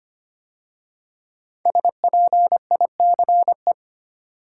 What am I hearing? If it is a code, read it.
Morse code, heard as a keyed tone, about 25 wpm, 700 Hz. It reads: SPICE
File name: 5 Letter Words - First 3 Letters Given - 25wpm